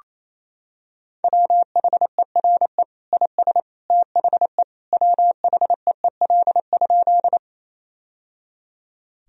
Morse code, heard as a keyed tone, about 28 wpm, 700 Hz. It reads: WHERE IS THE WHEEL?
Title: Sentences from Top 1000 Words - Review Part 3 - 28wpm